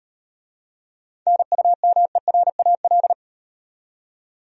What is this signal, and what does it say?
Morse code, heard as a keyed tone, about 38 wpm, 700 Hz. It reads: NUMERAL